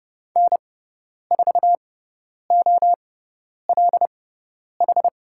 Morse code, heard as a keyed tone, about 30 wpm, spaced as for 10 wpm, 700 Hz. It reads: N4OLH
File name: Lesson 18g - Letter m - Call Signs 30wpm